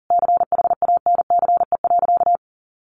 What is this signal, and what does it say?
Morse code, heard as a keyed tone, about 40 wpm, 700 Hz. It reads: CHANCE.
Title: Sentences from Top 1100 Words 40wpm